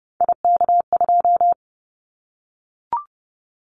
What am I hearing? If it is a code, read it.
Morse code, heard as a keyed tone, about 30 wpm, 700 Hz. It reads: IK2
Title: Callsign Prefixes for Top 50 DXCC - 30wpm